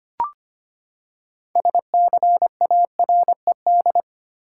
Morse code, heard as a keyed tone, about 25 wpm, 700 Hz. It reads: SCARED
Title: Six Letter Words - 25wpm